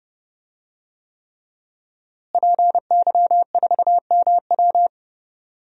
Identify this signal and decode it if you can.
Morse code, heard as a keyed tone, about 30 wpm, 700 Hz. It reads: PY4MW
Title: DXCC-100 Call Signs - 5 Characters - Repeated 3x - 30wpm